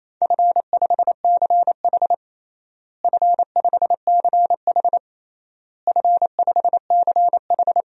Morse code, heard as a keyed tone, about 28 wpm, 700 Hz. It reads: F5CH F5CH F5CH
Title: DXCC-100 Call Signs - 4 Characters - Repeated 3x - 28wpm